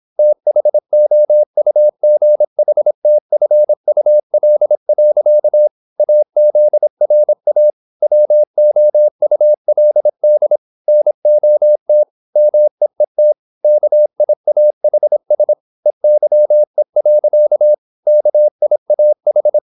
Morse code, heard as a keyed tone, about 26 wpm, 600 Hz. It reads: THOUGHTFUL. AZRA WOULD NOT MEET KIAHS EYE. KIAH
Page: Never 26wpm | CW med Gnister